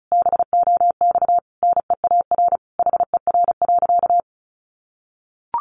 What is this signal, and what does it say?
Morse code, heard as a keyed tone, about 34 wpm, 700 Hz. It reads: BOX NEAR HER.
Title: Easy Phrases 35wpm